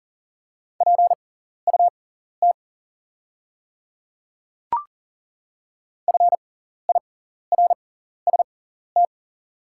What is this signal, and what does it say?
Morse code, heard as a keyed tone, about 40 wpm, spaced as for 14 wpm, 690 Hz. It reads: PUT FIRST
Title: 6x Character Spacing - Top 500 Words - 40wpm